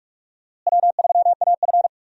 Morse code, heard as a keyed tone, about 44 wpm, 700 Hz. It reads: W3AF
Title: USA Call Signs - 4 Characters - Repeated 2x - 45wpm